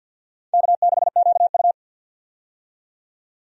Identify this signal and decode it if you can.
Morse code, heard as a keyed tone, about 51 wpm, 700 Hz. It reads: K6QU
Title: USA Call Signs - 4 Characters - 50wpm